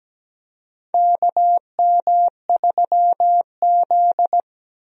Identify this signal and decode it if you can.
Morse code, heard as a keyed tone, about 17 wpm, 700 Hz. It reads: KM3Z
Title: USA Call Signs - 4 Characters - Repeated 2x - 17wpm